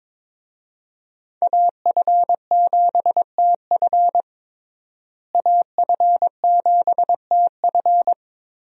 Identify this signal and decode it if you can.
Morse code, heard as a keyed tone, about 21 wpm, 700 Hz. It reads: AF7TF AF7TF
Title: USA Call Signs - 5 Characters - Repeated 2x - 22wpm